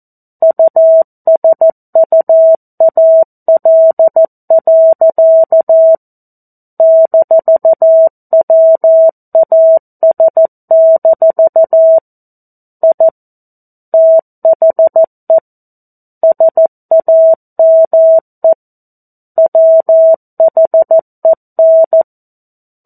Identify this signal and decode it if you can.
Morse code, heard as a keyed tone, about 14 wpm, 650 Hz. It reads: USUAL. -WAS- I THE SAME WHEN
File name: 02-Chapter-02 14 WPM